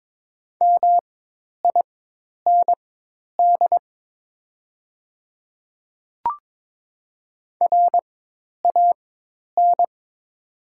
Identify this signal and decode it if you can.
Morse code, heard as a keyed tone, about 22 wpm, spaced as for 10 wpm, 700 Hz. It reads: MIND RAN
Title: 4x Character Spacing - Top 500 Words - 22wpm